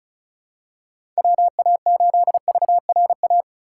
Morse code, heard as a keyed tone, about 34 wpm, 700 Hz. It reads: WA8VRA
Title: USA Call Signs - 6 Characters - 4x Character Spacing - 35wpm